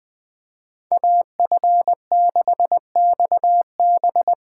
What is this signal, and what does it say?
Morse code, heard as a keyed tone, about 20 wpm, 700 Hz. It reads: AF6XB
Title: USA Call Signs - 5 Characters - 3x Character Spacing - 20wpm